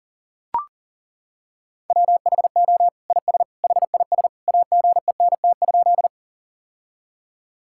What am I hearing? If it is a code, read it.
Morse code, heard as a keyed tone, about 40 wpm, 700 Hz. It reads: WHO IS HIS AGENT?
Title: Sentences from Top 1600 Words 40wpm